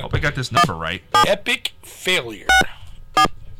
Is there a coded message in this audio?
Yes: DTMF from 0.6 to 3.3 s, heard as telephone touch tones, tones of about 140 ms, with speech. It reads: 17 64